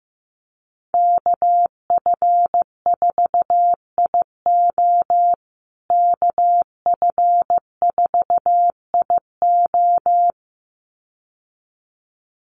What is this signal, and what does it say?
Morse code, heard as a keyed tone, about 15 wpm, 700 Hz. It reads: KF4IO KF4IO
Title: USA Call Signs Repeated 2x 15wpm